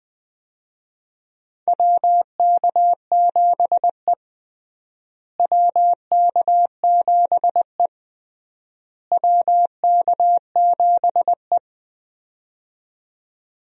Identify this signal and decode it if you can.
Morse code, heard as a keyed tone, about 20 wpm, 700 Hz. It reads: WK7E WK7E WK7E
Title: USA Call Signs - 4 Characters - Repeated 3x - 20wpm